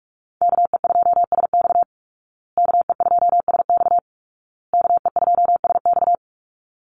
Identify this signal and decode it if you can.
Morse code, heard as a keyed tone, about 44 wpm, 700 Hz. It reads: KE2SX KE2SX KE2SX
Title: Call Signs Repeated 2x - 45wpm